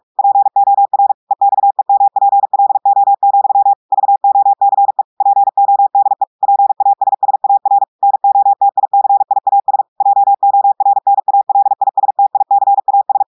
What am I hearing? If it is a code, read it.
Morse code, heard as a keyed tone, about 46 wpm, 800 Hz. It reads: POR EXEMPLO, VOCE PODE PASSAR NOTICIAS JORNALISTICAS
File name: audios / portugais / portugais texte en clair 2 vitesse 45 mots minutes